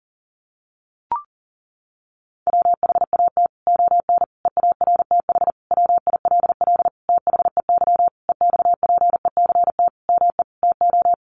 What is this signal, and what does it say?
Morse code, heard as a keyed tone, about 40 wpm, 700 Hz. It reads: WHAT ON EARTH WILL THEY EXPECT ME TO
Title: Sentences from Top 1100 Words 40wpm